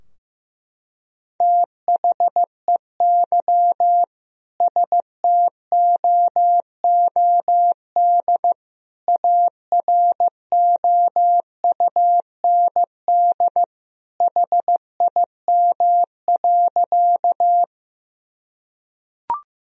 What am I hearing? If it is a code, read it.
Morse code, heard as a keyed tone, about 15 wpm, 700 Hz. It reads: THEY STOOD AROUND HIM.
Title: Sentences from Top 1000 Words - Review Part 3 - 15wpm